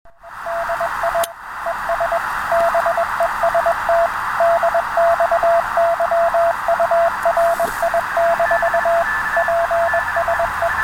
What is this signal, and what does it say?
Morse code, heard as a keyed tone, about 21 wpm, 680 Hz, with radio noise. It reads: DI ES BEST DX YURI - PSE